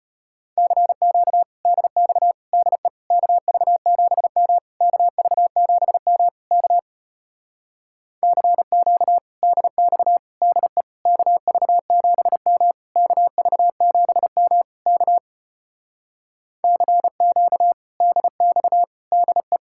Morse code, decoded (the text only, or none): CQ DX DE KV7M KV7M K CQ DX DE KV7M KV7M K CQ DX DE